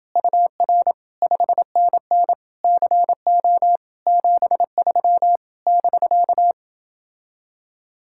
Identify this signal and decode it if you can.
Morse code, heard as a keyed tone, about 28 wpm, 700 Hz. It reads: UR 5NN CO 73 <BK>